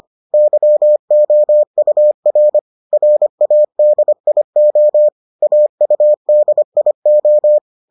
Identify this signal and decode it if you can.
Morse code, heard as a keyed tone, about 25 wpm, 600 Hz. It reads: YOUR RADIO AUDIO